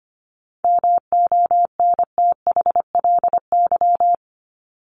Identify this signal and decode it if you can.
Morse code, heard as a keyed tone, about 25 wpm, 700 Hz. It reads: MONTHLY
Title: LY Suffix - Top 100 Words with Base Words 25wpm